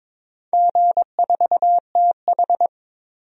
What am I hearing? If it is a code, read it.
Morse code, heard as a keyed tone, about 22 wpm, 700 Hz. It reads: G4TH
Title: DXCC-100 Call Signs - 4 Characters - 22wpm